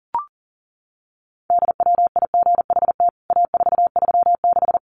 Morse code, heard as a keyed tone, about 40 wpm, 700 Hz. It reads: DWIGHT A436
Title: CWT Exchange 40wpm